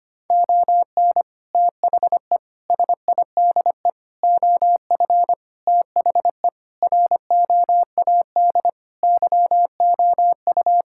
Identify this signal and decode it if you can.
Morse code, heard as a keyed tone, about 25 wpm, 700 Hz. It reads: ON THE SIDE OF THE ROAD YOU
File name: practice / 25wpm / Sentences with 3-Word Phrases - 25wpm